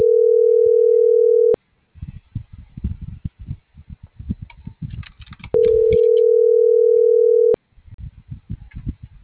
Ringback tone (North America), 440 + 480 Hz.